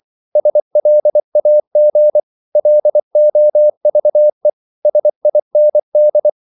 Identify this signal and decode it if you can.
Morse code, heard as a keyed tone, about 24 wpm, 600 Hz. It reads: SLAG LOVE SIND
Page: Korte danske ord 24wpm